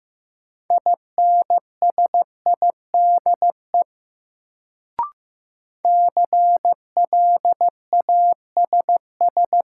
Morse code, heard as a keyed tone, about 15 wpm, 700 Hz. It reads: INSIDE CLASS
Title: Sentences from Top 1000 Words - Review Part 2 - New Word Drill - 15wpm